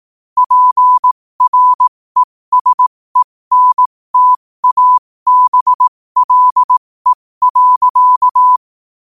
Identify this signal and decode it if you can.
Morse code, heard as a keyed tone, about 18 wpm, spaced as for 15 wpm, 1000 Hz. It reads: PRESENTABLE.
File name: qotdfiles / 15wpm